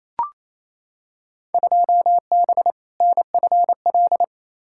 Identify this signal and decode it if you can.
Morse code, heard as a keyed tone, about 28 wpm, 700 Hz. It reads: 2B NFL
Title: Field Day Exchanges Encore - 28wpm